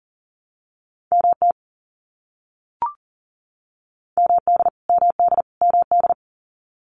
Morse code, heard as a keyed tone, about 40 wpm, 700 Hz. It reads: MT MD MD MD
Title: US States Repeated 3x 40wpm